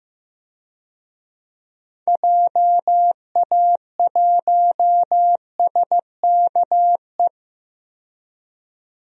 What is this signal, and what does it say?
Morse code, heard as a keyed tone, about 15 wpm, 700 Hz. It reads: JA1SKE
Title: DXCC-100 Call Signs - 6 Characters - 15wpm